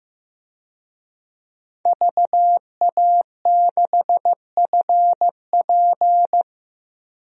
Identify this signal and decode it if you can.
Morse code, heard as a keyed tone, about 15 wpm, 700 Hz. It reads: VA6FP
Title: DXCC-100 Call Signs - 5 Characters - Repeated 2x - 15wpm